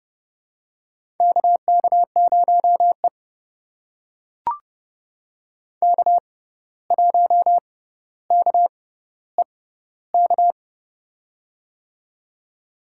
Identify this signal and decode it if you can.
Morse code, heard as a keyed tone, about 30 wpm, spaced as for 14 wpm, 700 Hz. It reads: KK0E K1KEK
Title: USA Call Signs - 6x Character Spacing - 30wpm